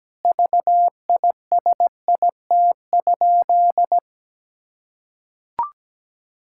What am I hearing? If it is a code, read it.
Morse code, heard as a keyed tone, about 17 wpm, 700 Hz. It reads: VISIT?
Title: Sentences from Top 1600 Words 17wpm